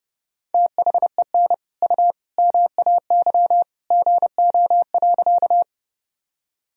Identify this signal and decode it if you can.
Morse code, heard as a keyed tone, about 30 wpm, 700 Hz. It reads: THEN U MAY GO.
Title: Sentences from Top 100 Words and Abbreviations 30wpm